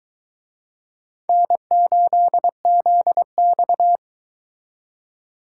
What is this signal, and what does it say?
Morse code, heard as a keyed tone, about 22 wpm, 700 Hz. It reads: N8ZX